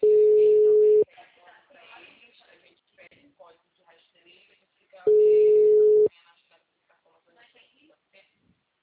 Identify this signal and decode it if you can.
Ringback tone (Europe, 425 Hz), with speech.
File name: out- -2033- - - .25059